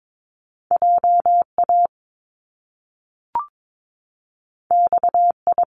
Morse code, heard as a keyed tone, about 22 wpm, 700 Hz. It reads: JA XI
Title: Top Two Letter Combinations in Words 22wpm